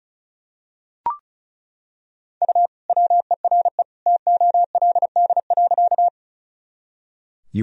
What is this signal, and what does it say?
Morse code, heard as a keyed tone, about 36 wpm, 700 Hz. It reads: U WERE TOLD.
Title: Sentences from Top 400 Words and Abbreviations 35wpm